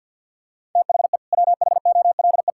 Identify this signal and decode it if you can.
Morse code, heard as a keyed tone, about 50 wpm, 700 Hz. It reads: THE WHOLE
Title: Sentences from Top 1000 Words - Review Part 5 - 50wpm